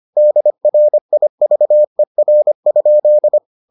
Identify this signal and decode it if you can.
Morse code, heard as a keyed tone, about 25 wpm, 600 Hz. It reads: DRIVER?